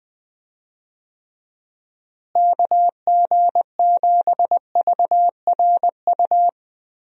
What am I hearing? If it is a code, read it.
Morse code, heard as a keyed tone, about 20 wpm, 700 Hz. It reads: KG7VRU
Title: USA Call Signs - 6 Characters - Repeated 3x - 20wpm